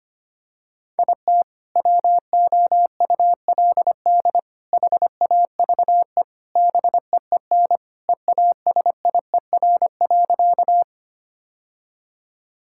Morse code, heard as a keyed tone, about 25 wpm, 700 Hz. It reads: IT WOULD HAVE BEEN EASIER.
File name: practice / 25wpm / Sentences with 3-Word Phrases - 25wpm